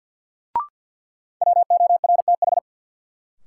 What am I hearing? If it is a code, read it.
Morse code, heard as a keyed tone, about 53 wpm, 700 Hz. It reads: WORTH
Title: Sentences from Top 800 Words - New Word Drill - 50wpm